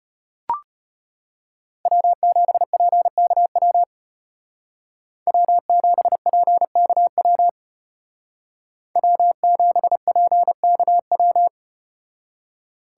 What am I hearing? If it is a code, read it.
Morse code, heard as a keyed tone, about 34 wpm, 700 Hz. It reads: W7PKW W7PKW W7PKW